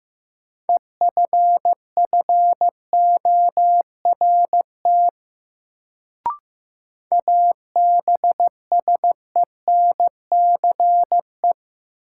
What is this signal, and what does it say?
Morse code, heard as a keyed tone, about 15 wpm, 700 Hz. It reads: EFFORT ABSENCE
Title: Sets of 4 Words - New Word Drill - 15wpm